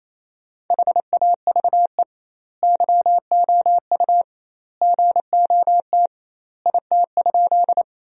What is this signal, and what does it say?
Morse code, heard as a keyed tone, about 28 wpm, 700 Hz. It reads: HAVE YOU GOT IT?
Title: 2x Word-Spacing - Sentences from Top 300 Words - 28wpm